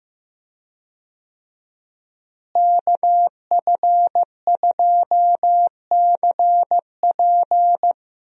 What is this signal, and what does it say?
Morse code, heard as a keyed tone, about 15 wpm, 700 Hz. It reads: KF2CP
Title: USA Call Signs - 5 Characters - Repeated 3x - 15wpm